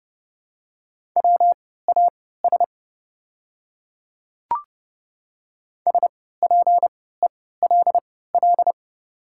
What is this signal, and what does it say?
Morse code, heard as a keyed tone, about 30 wpm, spaced as for 17 wpm, 690 Hz. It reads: WAS SPELL
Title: 3x Character Spacing - Top 200 Words - 30wpm